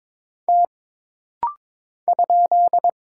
Morse code, heard as a keyed tone, about 22 wpm, 700 Hz. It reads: T ?
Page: This rapid-fire timing is geared to push your ICR (Instant Character Recognition) skills.